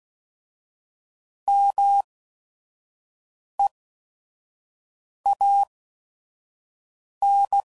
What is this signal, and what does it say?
Morse code, heard as a keyed tone, about 16 wpm, spaced as for 5 wpm, 800 Hz. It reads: MEAN